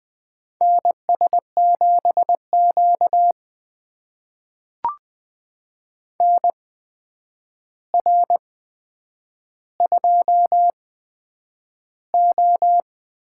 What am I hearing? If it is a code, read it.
Morse code, heard as a keyed tone, about 20 wpm, spaced as for 9 wpm, 700 Hz. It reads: NS7Q NR2O
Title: USA Call Signs - 4 Characters - 8x Character Spacing - 20wpm